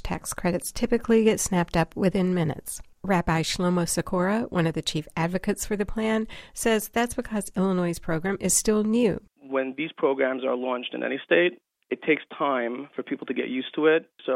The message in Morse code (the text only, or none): none